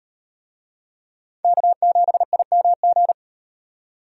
Morse code, heard as a keyed tone, about 36 wpm, 700 Hz. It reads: K7IMG